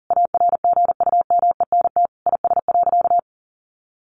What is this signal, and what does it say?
Morse code, heard as a keyed tone, about 40 wpm, 700 Hz. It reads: ARGUMENT IS.
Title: Sentences from Top 1400 Words 40wpm